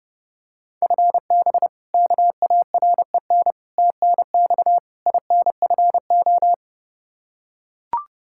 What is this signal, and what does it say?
Morse code, heard as a keyed tone, about 32 wpm, 700 Hz. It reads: FB KAREN TNX INFO
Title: K1USN SST QSO Practice 30wpm